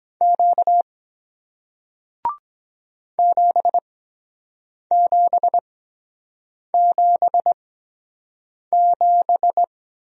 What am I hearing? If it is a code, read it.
Morse code, heard as a keyed tone, about 23 wpm, spaced as for 7 wpm, 700 Hz. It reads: Q 7777